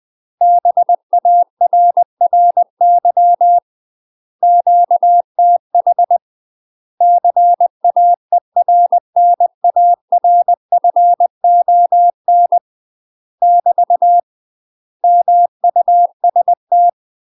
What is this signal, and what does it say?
Morse code, heard as a keyed tone, about 20 wpm, 700 Hz. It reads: BARRY QTH CAERNARFON <BT> MUST
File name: UK MASTER QSOs TRAINING / 20WPM / UK MASTER QSOs 0065